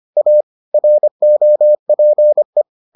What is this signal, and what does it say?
Morse code, heard as a keyed tone, about 25 wpm, 600 Hz. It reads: A ROPE